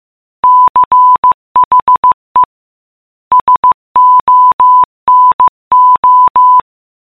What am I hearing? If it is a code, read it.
Morse code, heard as a keyed tone, about 15 wpm, 1000 Hz. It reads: CHE SONO
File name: prev